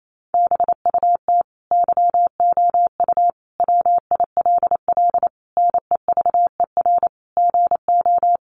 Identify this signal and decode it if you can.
Morse code, heard as a keyed tone, about 28 wpm, 700 Hz. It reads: BUT YOU WILL NEVER GO
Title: Sentences from Top 1500 Words 28wpm